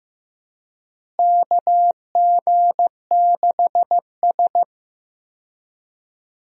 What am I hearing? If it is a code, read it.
Morse code, heard as a keyed tone, about 16 wpm, 700 Hz. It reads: KG6S
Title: USA Call Signs - 4 Characters - Repeated 3x - 15wpm